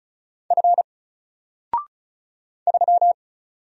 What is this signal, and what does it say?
Morse code, heard as a keyed tone, about 35 wpm, 700 Hz. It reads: F 3
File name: Single-Letter-Number-35wpm